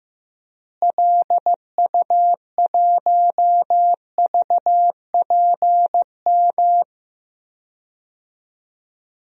Morse code, heard as a keyed tone, about 15 wpm, 700 Hz. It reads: LU1VPM
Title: DXCC-100 Call Signs - 6 Characters - Repeated 2x - 15wpm